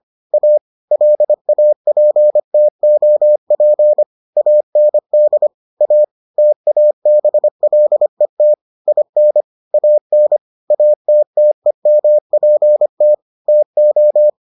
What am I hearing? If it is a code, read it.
Morse code, heard as a keyed tone, about 25 wpm, 600 Hz. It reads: A LAPTOP AND A TABLET IN AN ATTEMPT TO